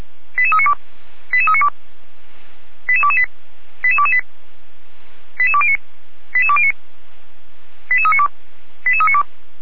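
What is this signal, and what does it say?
Five-tone selective call (ZVEI), heard as quick runs of five tones, tones of about 70 ms. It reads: 88392 88392 88228 88228 88209 88209 88392 88392
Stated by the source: ZVEI five-tone call 88193 received on SDR.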